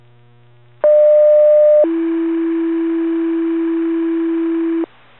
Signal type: two-tone page